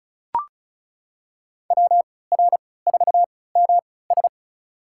Morse code, heard as a keyed tone, about 36 wpm, spaced as for 20 wpm, 700 Hz. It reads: WR4MS
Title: USA Call Signs - 5 Characters - 3x Character Spacing - 35wpm